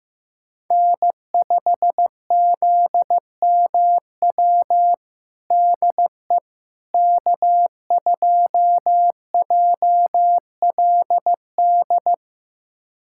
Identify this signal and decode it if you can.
Morse code, heard as a keyed tone, about 16 wpm, 700 Hz. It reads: N5ZMW DE K2JLD
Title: USA Callsign DE Callsign - 5 Characters - 15wpm